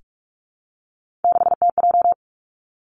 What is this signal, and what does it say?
Morse code, heard as a keyed tone, about 46 wpm, 700 Hz. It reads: 6TJ
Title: Three Challenging Letter-Number Combinations 45wpm